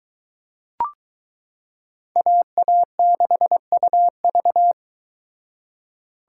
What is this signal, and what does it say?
Morse code, heard as a keyed tone, about 21 wpm, 700 Hz. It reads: AA6UV